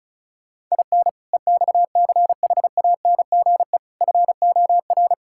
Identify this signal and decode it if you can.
Morse code, heard as a keyed tone, about 36 wpm, 700 Hz. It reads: IN EXCHANGE FOR
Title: Sentences from Top 1000 Words - Review Part 6 - 35wpm